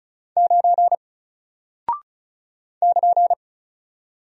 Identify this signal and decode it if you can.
Morse code, heard as a keyed tone, about 36 wpm, 700 Hz. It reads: 9 <KN>
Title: Single Character (Letter:Number:Punctuation:Prosign) Alt 35wpm